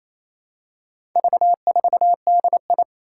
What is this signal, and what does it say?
Morse code, heard as a keyed tone, about 28 wpm, 700 Hz. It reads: V4DI
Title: Four Challenging Letter-Number Combinations 28wpm